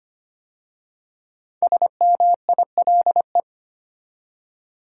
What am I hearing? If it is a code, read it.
Morse code, heard as a keyed tone, about 25 wpm, 700 Hz. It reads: SMILE
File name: 5 Letter Words - First 3 Letters Given - 25wpm